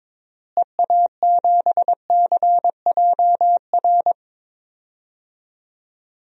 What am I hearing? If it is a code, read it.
Morse code, heard as a keyed tone, about 22 wpm, 700 Hz. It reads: EA7CJR
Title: DXCC-100 Call Signs - 6 Characters - Repeated 3x - 22wpm